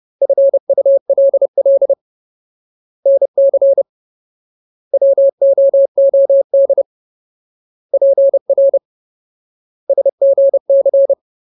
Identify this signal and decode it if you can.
Morse code, heard as a keyed tone, about 30 wpm, 550 Hz. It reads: FULL NC WOOD PR SGC
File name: Ultimate Word List / 30WPM Ultimate